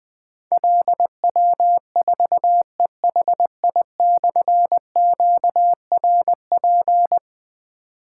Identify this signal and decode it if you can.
Morse code, heard as a keyed tone, about 20 wpm, 700 Hz. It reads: LW4EHI/QRP
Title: Difficult Call Signs 20wpm